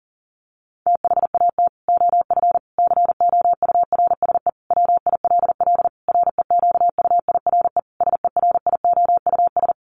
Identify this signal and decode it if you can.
Morse code, heard as a keyed tone, about 40 wpm, 700 Hz. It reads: THAT OF COURSE WILL REQUIRE SERIOUS
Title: Sentences from Top 1500 Words 40wpm